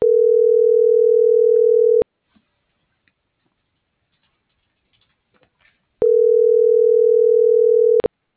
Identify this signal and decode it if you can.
Ringback tone (North America), 440 + 480 Hz.
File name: CENTRAL BLN / out ximena.diaz - .70654